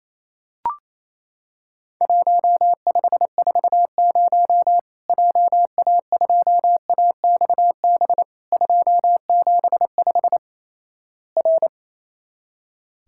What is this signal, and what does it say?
Morse code, heard as a keyed tone, about 28 wpm, 700 Hz. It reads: 1540 JA2AXB 275 R
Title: Worked All Europe DX Contest - QTC Exchange - 28wpm